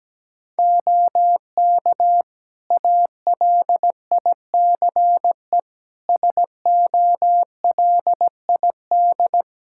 Morse code, heard as a keyed tone, about 17 wpm, 700 Hz. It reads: OK ALICE SOLID
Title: Practice QSO - Exchange 3 - More Info - 17wpm